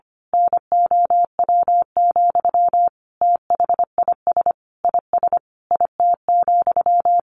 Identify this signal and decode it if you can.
Morse code, heard as a keyed tone, about 25 wpm, 700 Hz. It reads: NOW, THIS IS IT,
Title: A Sound Of Thunder 25wpm